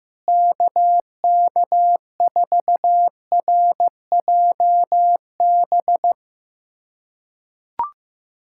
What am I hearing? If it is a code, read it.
Morse code, heard as a keyed tone, about 15 wpm, 700 Hz. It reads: KK4RJB
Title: USA Callsign DE Callsign - 6 Characters - 15wpm